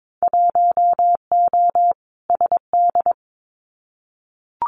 Morse code, heard as a keyed tone, about 22 wpm, 700 Hz. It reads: 1O SD
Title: Winter Field Day Exchanges Encore 22wpm